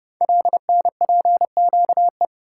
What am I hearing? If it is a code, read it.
Morse code, heard as a keyed tone, about 30 wpm, 700 Hz. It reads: LNPQE
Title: Code Groups - 5 Letters - 30wpm